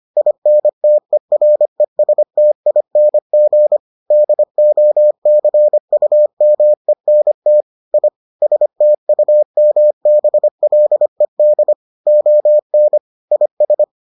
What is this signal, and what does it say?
Morse code, heard as a keyed tone, about 25 wpm, 600 Hz. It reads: INTERESTING DOCUMENT I STUMBLED ON IS